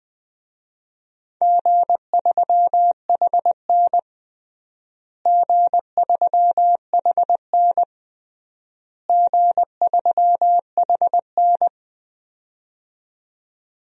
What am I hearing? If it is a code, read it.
Morse code, heard as a keyed tone, about 20 wpm, 700 Hz. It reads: G3HN G3HN G3HN
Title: DXCC-100 Call Signs - 4 Characters - Repeated 3x - 20wpm